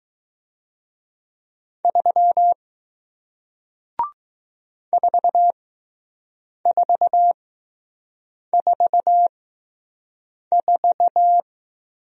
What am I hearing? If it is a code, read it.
Morse code, heard as a keyed tone, about 20 wpm, spaced as for 6 wpm, 700 Hz. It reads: 3 4444